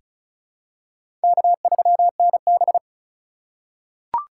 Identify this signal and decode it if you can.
Morse code, heard as a keyed tone, about 36 wpm, 700 Hz. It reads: K3NB
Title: USA Call Signs - 4 Characters - 6x Character Spacing - 35wpm